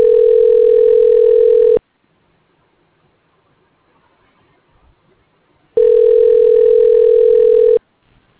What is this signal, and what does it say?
Ringback tone (North America), 440 + 480 Hz.